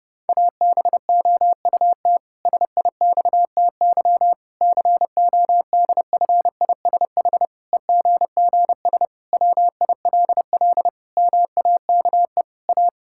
Morse code, heard as a keyed tone, about 30 wpm, 700 Hz. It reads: ABOUT SIXTY CODFISH EGGS WILL MAKE A
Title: Pangram Challenge v2 - 30wpm